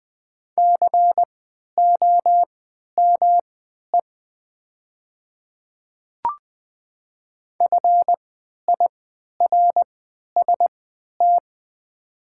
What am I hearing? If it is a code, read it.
Morse code, heard as a keyed tone, about 20 wpm, spaced as for 11 wpm, 700 Hz. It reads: COME FIRST
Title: 3x Character Spacing - Top 200 Words - 20wpm